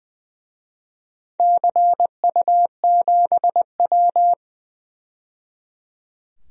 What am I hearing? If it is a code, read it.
Morse code, heard as a keyed tone, about 20 wpm, 700 Hz. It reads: CU7W
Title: DXCC-100 Call Signs - 4 Characters - 20wpm